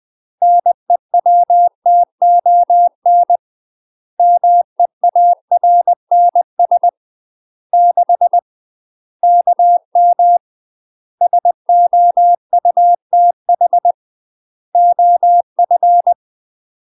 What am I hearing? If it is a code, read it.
Morse code, heard as a keyed tone, about 20 wpm, 700 Hz. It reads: NEWTON MEARNS 6 KM SOUTH OF
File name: UK MASTER QSOs TRAINING / 20WPM / UK MASTER QSOs 0040